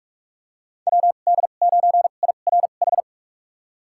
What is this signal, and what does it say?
Morse code, heard as a keyed tone, about 44 wpm, spaced as for 32 wpm, 700 Hz. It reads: WD9IRH
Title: USA Call Signs - 2x Character Spacing - 45wpm